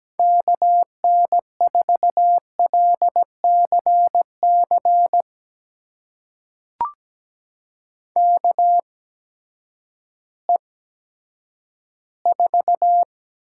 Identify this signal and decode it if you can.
Morse code, heard as a keyed tone, about 17 wpm, spaced as for 10 wpm, 700 Hz. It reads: KN4LCC KE4
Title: USA Call Signs - 6 Characters - 8x Character Spacing - 17wpm